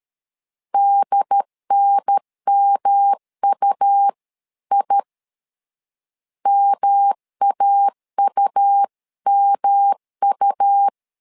Morse code, heard as a keyed tone, about 13 wpm, 800 Hz. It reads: DNMUI MAUMU
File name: LEZIONI / 3 MWRO / 3 M 12wpm 1541